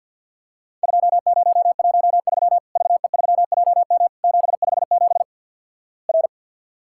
Morse code, heard as a keyed tone, about 48 wpm, 700 Hz. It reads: 2013 VE3JM 757 R
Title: Worked All Europe DX Contest - QTC Exchange - 50wpm